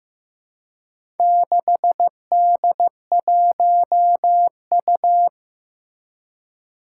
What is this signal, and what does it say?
Morse code, heard as a keyed tone, about 15 wpm, 700 Hz. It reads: 6D1U
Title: Four Challenging Letter-Number Combinations 15wpm